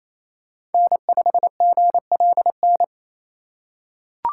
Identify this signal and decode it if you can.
Morse code, heard as a keyed tone, about 28 wpm, 700 Hz. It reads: N5GLN
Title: USA Call Signs - 5 Characters - 4x Character Spacing - 28wpm